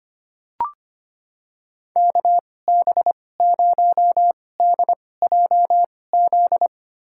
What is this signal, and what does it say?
Morse code, heard as a keyed tone, about 25 wpm, spaced as for 18 wpm, 700 Hz. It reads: KB0DJZ
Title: USA Call Signs - 2x Character Spacing - 25wpm